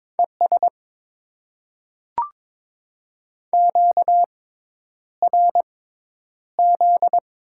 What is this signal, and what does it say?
Morse code, heard as a keyed tone, about 22 wpm, spaced as for 10 wpm, 700 Hz. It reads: ES QRZ
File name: Top 90 QSO Elements - 6x Character Spacing - 22wpm